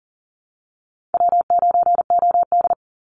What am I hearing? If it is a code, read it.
Morse code, heard as a keyed tone, about 40 wpm, 700 Hz. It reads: W9OD
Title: Easy Call Signs Repeated 2x - 40wpm